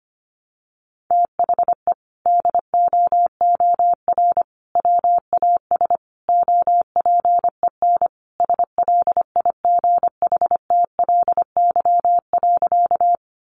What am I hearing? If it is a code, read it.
Morse code, heard as a keyed tone, about 25 wpm, 700 Hz. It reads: THE DOOR WAS OPEN SLIGHTLY.
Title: Sentences from Top 1300 Words 25wpm